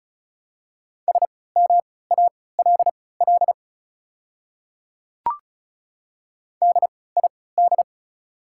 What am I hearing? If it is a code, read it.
Morse code, heard as a keyed tone, about 34 wpm, spaced as for 20 wpm, 690 Hz. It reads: SMALL DID
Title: 3x Character Spacing - Top 500 Words - 35wpm